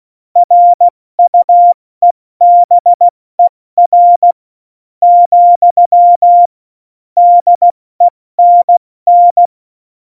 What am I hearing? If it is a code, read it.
Morse code, heard as a keyed tone, about 16 wpm, 700 Hz. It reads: RUEBER , DENN
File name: Morsekurs / Witze / Witz 30. Oktober 2018, 70 F80